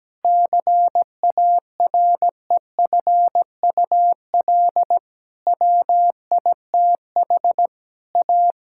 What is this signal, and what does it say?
Morse code, heard as a keyed tone, about 17 wpm, 700 Hz. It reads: CAREFUL WITH A
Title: Sentences from Top 1600 Words 17wpm